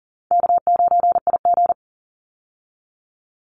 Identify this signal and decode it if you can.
Morse code, heard as a keyed tone, about 40 wpm, 700 Hz. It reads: K9IG
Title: Call Signs Repeated 2x - 40wpm